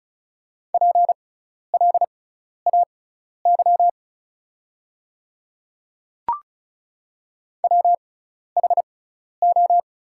Morse code, heard as a keyed tone, about 36 wpm, spaced as for 12 wpm, 700 Hz. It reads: PLAY WHO
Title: 6x Character Spacing - Top 200 Words - 35wpm